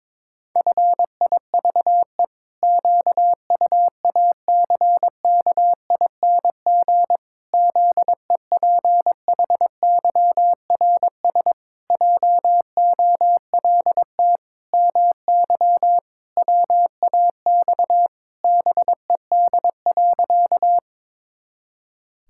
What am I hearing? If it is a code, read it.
Morse code, heard as a keyed tone, about 22 wpm, 700 Hz. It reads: FIVE QUACKING ZEPHYRS JOLT MY WAX BED.
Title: Pangram Challenge v2 - 22wpm